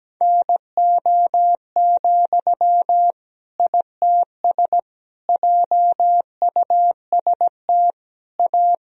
Morse code, heard as a keyed tone, about 17 wpm, 700 Hz. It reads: NO, ITS JUST A
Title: Sentences from Top 1000 Words - Review Part 3 - 17wpm